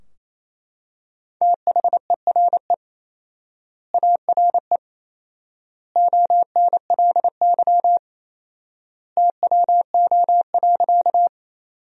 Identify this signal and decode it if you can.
Morse code, heard as a keyed tone, about 28 wpm, 700 Hz. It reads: THERE ARE ONLY TWO.
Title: 4x Word-Spacing - Sentences from Top 200 Words - 28wpm